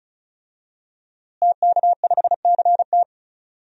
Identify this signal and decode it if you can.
Morse code, heard as a keyed tone, about 34 wpm, 700 Hz. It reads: TK5CT
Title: DXCC-100 Call Signs - 5 Characters - Repeated 2x - 35wpm